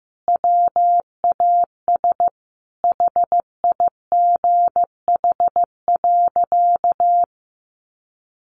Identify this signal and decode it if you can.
Morse code, heard as a keyed tone, about 15 wpm, 700 Hz. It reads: WAS HIGH.
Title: Sentences from Top 1400 Words 15wpm